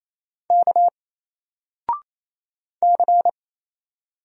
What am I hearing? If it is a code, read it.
Morse code, heard as a keyed tone, about 27 wpm, 700 Hz. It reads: K C
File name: Single-Letter-Number-28wpm